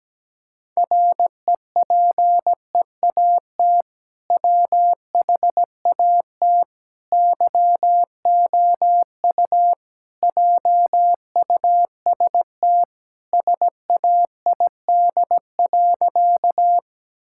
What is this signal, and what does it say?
Morse code, heard as a keyed tone, about 17 wpm, 700 Hz. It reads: REPEAT WHAT YOU JUST SAID.
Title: Sentences from Top 1600 Words 17wpm